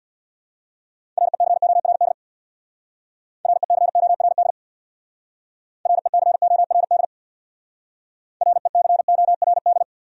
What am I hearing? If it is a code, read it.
Morse code, heard as a keyed tone, about 61 wpm, 690 Hz. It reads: RECORD RECORD RECORD RECORD
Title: Speed-Racing Top 500 Words 50wpm